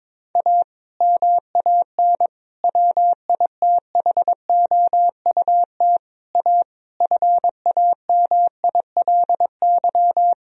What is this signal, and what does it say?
Morse code, heard as a keyed tone, about 22 wpm, 700 Hz. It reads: A MAN WITHOUT A FAMILY
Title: Sentences from Top 1000 Words - Review Part 3 - 22wpm